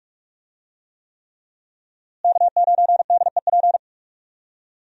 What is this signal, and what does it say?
Morse code, heard as a keyed tone, about 41 wpm, 700 Hz. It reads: K9DEP
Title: USA Call Signs - 5 Characters - Repeated 3x - 45wpm